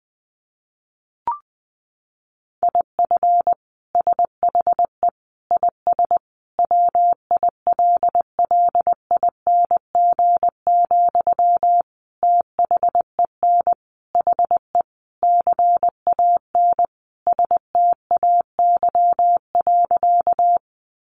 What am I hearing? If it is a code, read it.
Morse code, heard as a keyed tone, about 20 wpm, 700 Hz. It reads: IF SHE IS WILLING, THEN HE CAN STAY.
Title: Sentences from Top 1000 Words 20wpm